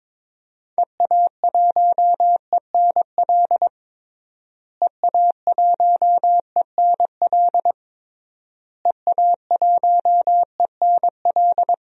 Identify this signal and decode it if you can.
Morse code, heard as a keyed tone, about 22 wpm, 700 Hz. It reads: EA1ENL EA1ENL EA1ENL
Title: DXCC-100 Call Signs - 6 Characters - Repeated 3x - 22wpm